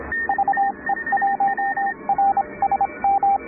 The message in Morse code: VE1RSM